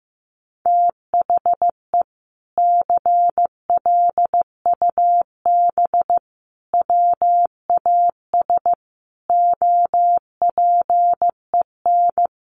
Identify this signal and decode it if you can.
Morse code, heard as a keyed tone, about 15 wpm, 700 Hz. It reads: THE CLUB WAS OPEN
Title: Sentences from Top 1200 Words 15wpm